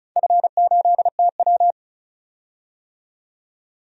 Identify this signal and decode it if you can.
Morse code, heard as a keyed tone, about 36 wpm, 700 Hz. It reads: F8TW
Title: DXCC-100 Call Signs - 4 Characters - 35wpm